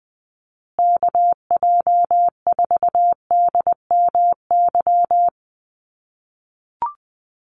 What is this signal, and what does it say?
Morse code, heard as a keyed tone, about 20 wpm, 700 Hz. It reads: KJ4DMY
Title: USA Call Signs Repeated 2x 20wpm